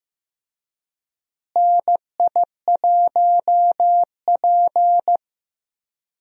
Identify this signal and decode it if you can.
Morse code, heard as a keyed tone, about 15 wpm, 700 Hz. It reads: NI1P
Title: USA Call Signs - 4 Characters - Repeated 3x - 15wpm